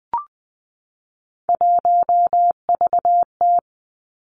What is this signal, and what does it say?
Morse code, heard as a keyed tone, about 20 wpm, 700 Hz. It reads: 1VT
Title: Three Challenging Letter-Number Combinations 20wpm